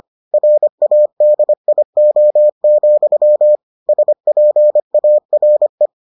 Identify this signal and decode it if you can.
Morse code, heard as a keyed tone, about 25 wpm, 600 Hz. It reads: RADIO, SPARE